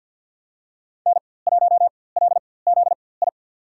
Morse code, heard as a keyed tone, about 48 wpm, spaced as for 23 wpm, 700 Hz. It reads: N1LZI
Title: USA Call Signs - 4x Character Spacing - 50wpm